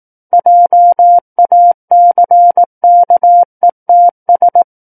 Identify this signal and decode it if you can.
Morse code, heard as a keyed tone, about 18 wpm, 700 Hz. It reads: JACKETS
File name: 18 WPM